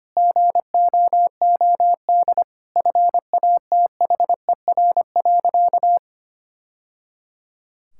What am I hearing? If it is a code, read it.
Morse code, heard as a keyed tone, about 25 wpm, 700 Hz. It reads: GOOD FATHER.
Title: Sentences from Top 1000 Words - Review Part 5 - 25wpm